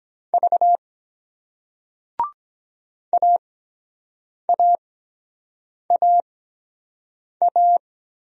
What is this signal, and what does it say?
Morse code, heard as a keyed tone, about 23 wpm, spaced as for 7 wpm, 700 Hz. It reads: V AAAA